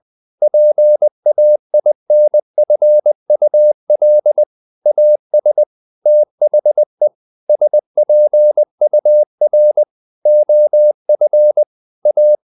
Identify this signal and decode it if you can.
Morse code, heard as a keyed tone, about 20 wpm, 600 Hz. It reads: PAINFUL AS THE SPUR OF A
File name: the old man and the sea / the old man and the sea part 1 20wpm